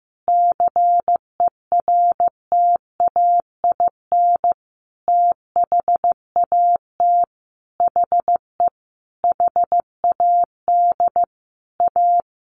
Morse code, heard as a keyed tone, about 14 wpm, 700 Hz. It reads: CERTAIN THAT HE HAD A
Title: Sentences from Top 1500 Words 15wpm